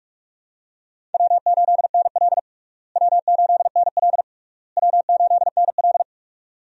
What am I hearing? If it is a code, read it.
Morse code, heard as a keyed tone, about 46 wpm, 700 Hz. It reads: W8NL W8NL W8NL
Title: USA Call Signs - 4 Characters - Repeated 3x - 45wpm